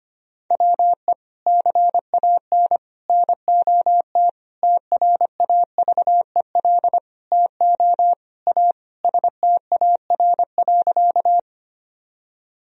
Morse code, heard as a keyed tone, about 25 wpm, 700 Hz. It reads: WE CAN NOT TRAVEL TO A STAR.
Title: Sentences from Top 1000 Words - Review Part 2 - 25wpm